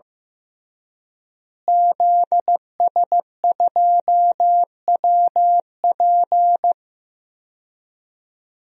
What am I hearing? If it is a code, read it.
Morse code, heard as a keyed tone, about 15 wpm, 700 Hz. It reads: ZS2WP
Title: DXCC-100 Call Signs - 5 Characters - Repeated 3x - 15wpm